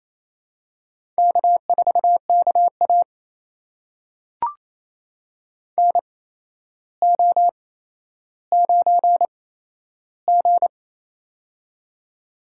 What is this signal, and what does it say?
Morse code, heard as a keyed tone, about 28 wpm, spaced as for 12 wpm, 700 Hz. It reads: K4KA NO9G
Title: USA Call Signs - 4 Characters - 8x Character Spacing - 28wpm